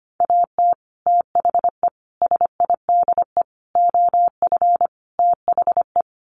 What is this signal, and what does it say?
Morse code, heard as a keyed tone, about 25 wpm, 700 Hz. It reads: AT THE SIDE OF THE
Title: Sets of 5 Words Encore2 25wpm